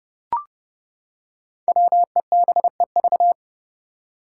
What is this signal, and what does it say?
Morse code, heard as a keyed tone, about 32 wpm, 700 Hz. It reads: WEBEV
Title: Five Challenging Letter-Number Combinations 30wpm